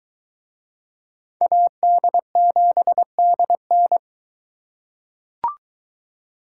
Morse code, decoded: AD7DN

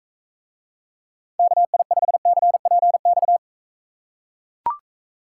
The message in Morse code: KI5CPX

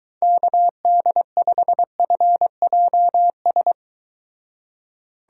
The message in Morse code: KD5FJS

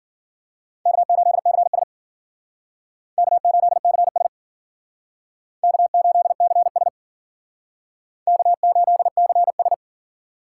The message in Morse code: K8CS K8CS K8CS K8CS

